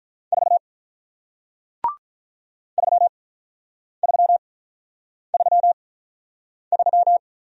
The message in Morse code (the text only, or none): 4 3333